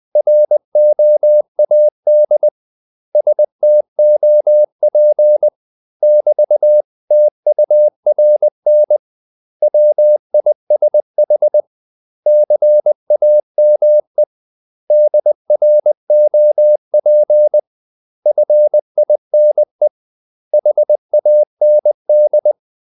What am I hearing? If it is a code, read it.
Morse code, heard as a keyed tone, about 20 wpm, 600 Hz. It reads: ROAD STOP <BT>TURN WISH CAME DROP FINE HAND
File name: CW / Practice / 500MostCommonEnglishWords20